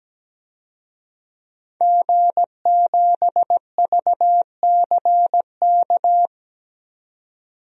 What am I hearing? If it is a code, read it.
Morse code, heard as a keyed tone, about 17 wpm, 700 Hz. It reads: G7VCK
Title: DXCC-100 Call Signs - 5 Characters - 17wpm